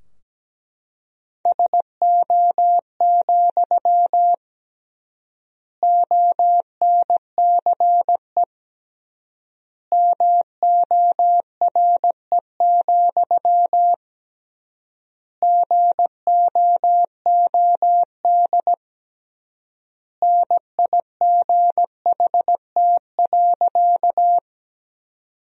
Morse code, heard as a keyed tone, about 17 wpm, 700 Hz. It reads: SO, ONCE MORE, GOOD NIGHT.
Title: 3x Word-Spacing - Sentences from Top 300 Words - 17wpm